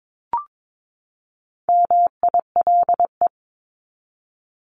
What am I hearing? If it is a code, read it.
Morse code, heard as a keyed tone, about 22 wpm, 700 Hz. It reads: MILE
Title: ES:S Suffix - Top 100 Words with Base Words 22wpm